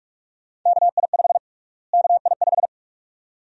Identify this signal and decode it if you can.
Morse code, heard as a keyed tone, about 42 wpm, 700 Hz. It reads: KI5 KI5
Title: USA Call Sign Prefixes - Repeated 2x - 45wpm